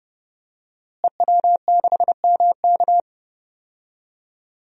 Morse code, heard as a keyed tone, about 30 wpm, 700 Hz. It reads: EW6MK
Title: DXCC-100 Call Signs - 5 Characters - Repeated 3x - 30wpm